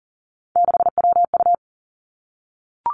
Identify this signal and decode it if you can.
Morse code, heard as a keyed tone, about 42 wpm, 700 Hz. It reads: 6WU
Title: Three Challenging Letter-Number Combinations 40wpm